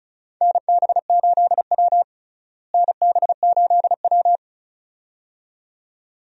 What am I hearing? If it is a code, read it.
Morse code, heard as a keyed tone, about 36 wpm, 700 Hz. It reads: NB8W NB8W
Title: USA Call Signs - 4 Characters - Repeated 2x - 35wpm